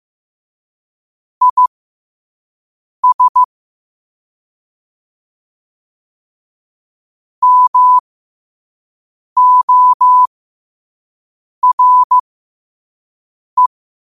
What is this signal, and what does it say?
Morse code, heard as a keyed tone, about 15 wpm, spaced as for 5 wpm, 1000 Hz. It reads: IS MORE
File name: qotdfiles / 5wpm